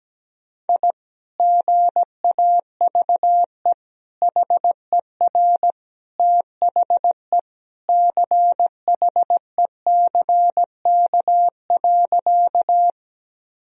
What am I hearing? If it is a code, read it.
Morse code, heard as a keyed tone, about 17 wpm, 700 Hz. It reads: I GAVE HER THE CHECK.
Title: Sentences from Top 1000 Words - Review Part 3 - 17wpm